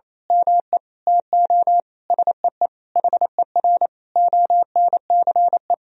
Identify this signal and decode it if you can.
Morse code, heard as a keyed tone, about 28 wpm, 700 Hz. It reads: ME TO SEE HER ONCE
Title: Sentences from Top 500 Words and Abbreviations 28wpm